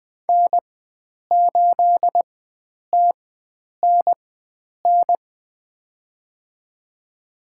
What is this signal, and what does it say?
Morse code, heard as a keyed tone, about 21 wpm, spaced as for 9 wpm, 700 Hz. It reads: N8TNN
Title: USA Call Signs - 5 Characters - 4x Character Spacing - 20wpm